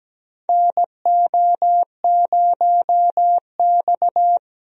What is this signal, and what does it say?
Morse code, heard as a keyed tone, about 17 wpm, 700 Hz. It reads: NO0X
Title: USA Call Signs - 4 Characters - 17wpm